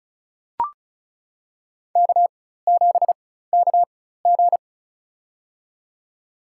Morse code, heard as a keyed tone, about 36 wpm, spaced as for 16 wpm, 700 Hz. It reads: K7KG
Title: USA Call Signs - 4 Characters - 4x Character Spacing - 35wpm